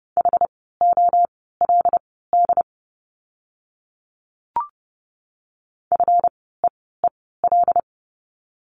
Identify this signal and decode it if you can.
Morse code, heard as a keyed tone, about 29 wpm, spaced as for 17 wpm, 700 Hz. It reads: HOLD FEEL
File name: course / 30:10wpm / 237 Lesson 38f - Letter q - Words 30wpm